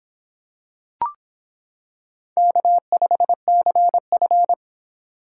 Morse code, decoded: K5CF

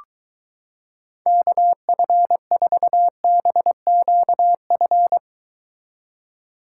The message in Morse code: KF4BQF

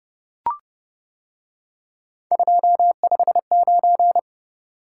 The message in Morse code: 259

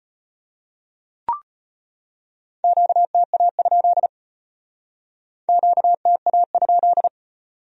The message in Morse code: QTA? QTA?

